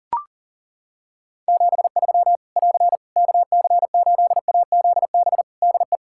Morse code, heard as a keyed tone, about 40 wpm, 700 Hz. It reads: 73 <AR> KC8AZB DE
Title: Practice QSO - Exchange 4 - Ending - 40wpm